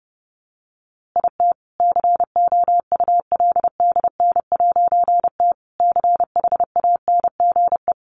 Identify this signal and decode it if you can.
Morse code, heard as a keyed tone, about 30 wpm, 700 Hz. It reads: IT COULDN'T CHANGE
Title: A Sound Of Thunder 30wpm